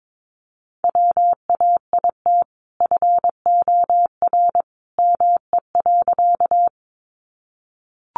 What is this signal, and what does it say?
Morse code, heard as a keyed tone, about 22 wpm, 700 Hz. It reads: WAIT FOR ME.
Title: Sentences from Top 500 Words 22wpm